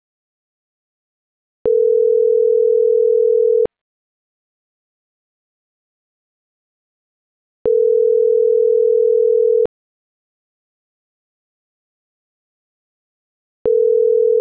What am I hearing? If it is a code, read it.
Ringback tone (North America), 440 + 480 Hz.